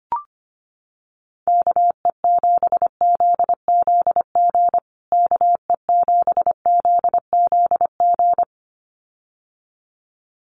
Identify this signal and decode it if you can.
Morse code, heard as a keyed tone, about 25 wpm, 700 Hz. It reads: KE7ZZG KE7ZZG
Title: USA Call Signs Repeated 2x 25wpm